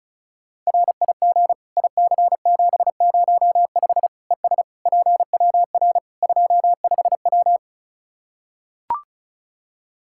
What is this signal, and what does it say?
Morse code, heard as a keyed tone, about 36 wpm, 700 Hz. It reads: RIG IC705 ES PWR 25W
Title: Practice QSO - Exchange 2 - Info - 35wpm